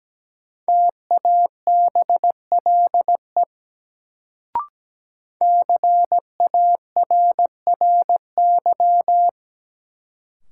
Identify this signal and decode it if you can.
Morse code, heard as a keyed tone, about 17 wpm, 700 Hz. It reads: TABLE CARRY
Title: Sentences from Top 400 Words - New Word Drill - 17wpm